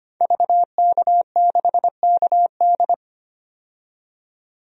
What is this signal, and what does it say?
Morse code, heard as a keyed tone, about 25 wpm, 700 Hz. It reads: VK6KD
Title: DXCC-100 Call Signs - 5 Characters - 25wpm